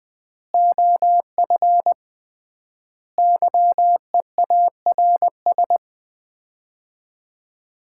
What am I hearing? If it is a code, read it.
Morse code, heard as a keyed tone, about 21 wpm, 700 Hz. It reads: OF YEARS
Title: 3x Word-Spacing - Sets of 4 Words - 20wpm